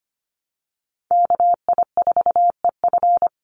Morse code, heard as a keyed tone, about 25 wpm, 700 Hz. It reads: KI4EF
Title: USA Call Signs Repeated 2x 25wpm